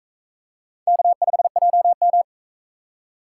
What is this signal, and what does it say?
Morse code, heard as a keyed tone, about 43 wpm, 700 Hz. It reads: K5JM